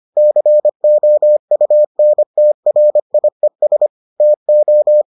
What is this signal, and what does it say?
Morse code, heard as a keyed tone, about 25 wpm, 600 Hz. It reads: COUNTRIES TO